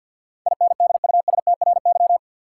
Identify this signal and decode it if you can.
Morse code, heard as a keyed tone, about 47 wpm, 700 Hz. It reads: INDUSTRY
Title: Sentences from Top 1000 Words - Review Part 4 - New Word Drill - 50wpm